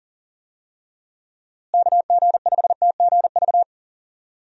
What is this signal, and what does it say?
Morse code, heard as a keyed tone, about 40 wpm, 700 Hz. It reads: KG5TGV
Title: USA Call Signs - 6 Characters - Repeated 3x - 40wpm